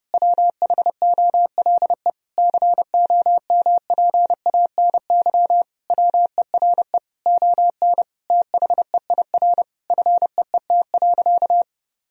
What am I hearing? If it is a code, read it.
Morse code, heard as a keyed tone, about 30 wpm, 700 Hz. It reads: WHOLE COMPANY WERE ON THEIR FEET.
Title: Sentences from Top 1000 Words - Review Part 3 - 30wpm